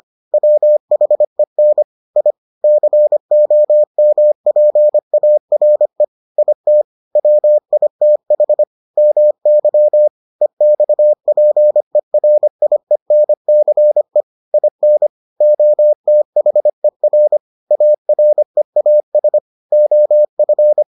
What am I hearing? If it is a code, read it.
Morse code, heard as a keyed tone, about 25 wpm, 600 Hz. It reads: WHEN I COMPARE IT WITH MY EXPERIENCE IN OTHER AREAS OF